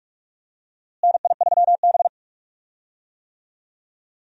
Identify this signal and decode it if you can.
Morse code, heard as a keyed tone, about 43 wpm, 700 Hz. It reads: NI3B
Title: USA Call Signs - 4 Characters - Repeated 2x - 45wpm